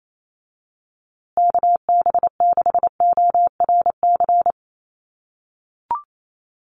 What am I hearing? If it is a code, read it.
Morse code, heard as a keyed tone, about 28 wpm, 700 Hz. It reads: KB6ORC
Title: USA Call Signs Repeated 2x 28wpm